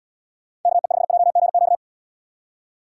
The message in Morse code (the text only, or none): KE5JGQ